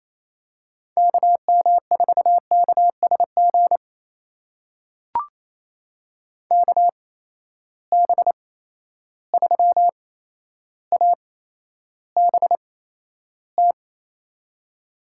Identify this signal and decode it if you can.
Morse code, heard as a keyed tone, about 28 wpm, spaced as for 12 wpm, 700 Hz. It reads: KM4KSG KB3ABT
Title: USA Call Signs - 6 Characters - 8x Character Spacing - 28wpm